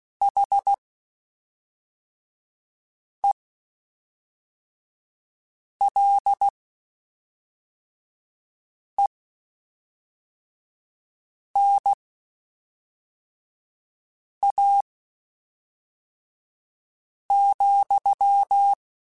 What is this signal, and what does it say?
Morse code, heard as a keyed tone, about 16 wpm, spaced as for 3 wpm, 800 Hz. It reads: HELENA,
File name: QSOs / QSOs-mp3s / WD0BGP 1@5